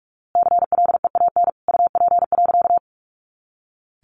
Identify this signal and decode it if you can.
Morse code, heard as a keyed tone, about 46 wpm, 700 Hz. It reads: CLEAN UP.
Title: A Sound Of Thunder 45wpm